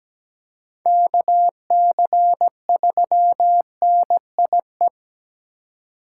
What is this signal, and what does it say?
Morse code, heard as a keyed tone, about 17 wpm, 700 Hz. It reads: KC3NIE
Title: K1USN SST QSO Practice - 17wpm